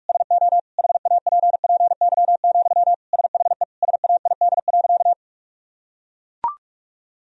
Morse code, heard as a keyed tone, about 41 wpm, 700 Hz. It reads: SO HAPPY, SHE SAID.
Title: Sentences from Top 1000 Words - Review Part 1 - 45wpm